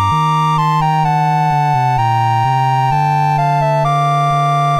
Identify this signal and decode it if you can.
There is no code.